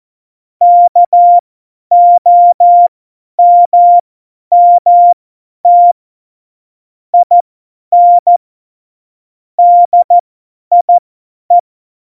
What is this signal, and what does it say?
Morse code, heard as a keyed tone, about 14 wpm, spaced as for 10 wpm, 700 Hz. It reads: KOMMT IN DIE
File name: Morsekurs / Witze / Witz 22. Januar 2019, 50 F70